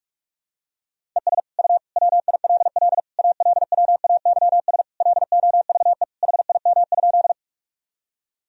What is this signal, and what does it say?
Morse code, heard as a keyed tone, about 43 wpm, 700 Hz. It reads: ES U WILL ALWAYS LOVE HIM?
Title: Sentences from Top 400 Words and Abbreviations 45wpm